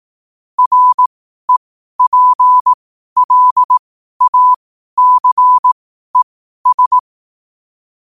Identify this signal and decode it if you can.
Morse code, heard as a keyed tone, about 18 wpm, spaced as for 12 wpm, 1000 Hz. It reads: REPLACES
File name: qotdfiles / 12wpm